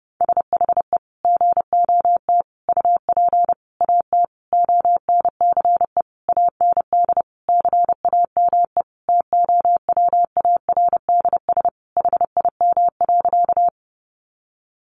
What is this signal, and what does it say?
Morse code, heard as a keyed tone, about 30 wpm, 700 Hz. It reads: SHE GOT UP AT ONCE AND CAME TOWARDS HIM.
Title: Sentences from Top 1100 Words 30wpm